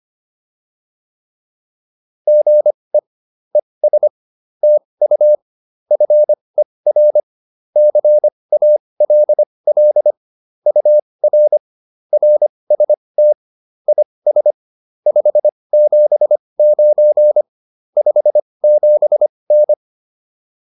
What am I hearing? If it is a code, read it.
Morse code, heard as a keyed tone, about 25 wpm, spaced as for 20 wpm, 600 Hz. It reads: GE ES TU FER CALL UR RST IS 579 57N